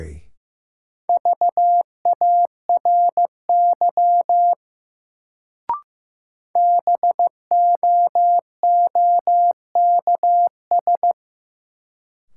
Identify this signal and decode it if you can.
Morse code, heard as a keyed tone, about 16 wpm, 700 Hz. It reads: VARY BOOKS
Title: Sentences from Top 1200 Words - New Word Drill Alt - 15wpm